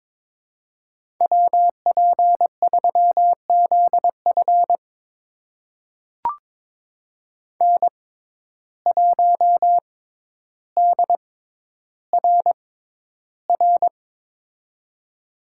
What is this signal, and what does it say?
Morse code, heard as a keyed tone, about 23 wpm, spaced as for 11 wpm, 700 Hz. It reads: WP3ZF N1DRR
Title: USA Call Signs - 5 Characters - 6x Character Spacing - 22wpm